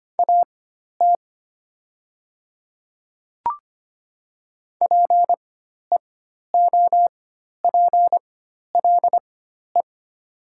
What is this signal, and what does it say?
Morse code, heard as a keyed tone, about 25 wpm, spaced as for 12 wpm, 700 Hz. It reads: AT PEOPLE
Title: 4x Character Spacing - Top 100 Words - 25wpm